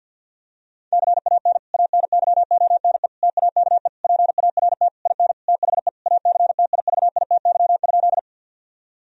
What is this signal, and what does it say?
Morse code, heard as a keyed tone, about 50 wpm, 700 Hz. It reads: CAN ANYONE TAKE PART IN THE ACTIVITY?
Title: Sentences from Top 1000 Words - Review Part 4 - 50wpm